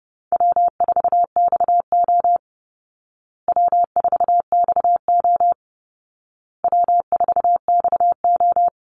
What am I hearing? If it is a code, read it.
Morse code, heard as a keyed tone, about 30 wpm, 700 Hz. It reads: W4XO W4XO W4XO
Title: Easy Call Signs Repeated 3x - 30wpm